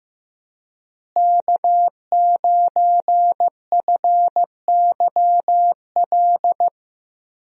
Morse code, heard as a keyed tone, about 15 wpm, 700 Hz. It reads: K9FYL
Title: USA Call Signs - 5 Characters - Repeated 2x - 15wpm